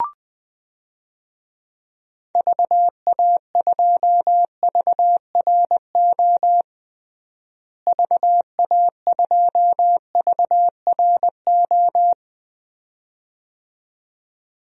Morse code, heard as a keyed tone, about 21 wpm, 700 Hz. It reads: VA2VRO VA2VRO
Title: DXCC-100 Call Signs - 6 Characters - Repeated 2x - 20wpm